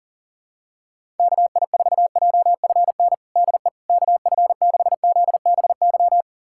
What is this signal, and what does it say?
Morse code, heard as a keyed tone, about 40 wpm, 700 Hz. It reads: KI4JFN DE KF6ZBY
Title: USA Callsign DE Callsign - 6 Characters - 40wpm